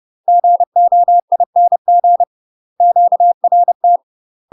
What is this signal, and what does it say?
Morse code, heard as a keyed tone, about 30 wpm, 700 Hz. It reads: GOING QRT
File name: UK MASTER QSOs TRAINING / 30WPM / UK MASTER QSOs 0032